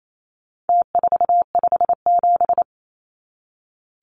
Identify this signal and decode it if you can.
Morse code, heard as a keyed tone, about 28 wpm, 700 Hz. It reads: T457
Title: Numbers - Contest Style - 4-Digit 28wpm